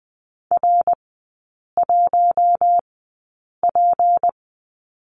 Morse code, heard as a keyed tone, about 20 wpm, spaced as for 8 wpm, 700 Hz. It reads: R1P
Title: 205 Lesson 33e - Letter j - Three Characters 20wpm